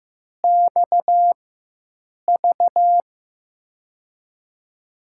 Morse code, heard as a keyed tone, about 15 wpm, spaced as for 7 wpm, 700 Hz. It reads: XV
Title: USA Call Signs - 6 Characters - 4x Character Spacing - 15wpm